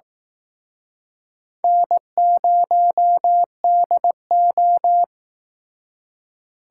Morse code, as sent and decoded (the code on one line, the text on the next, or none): -. ----- -.. ---
N0DO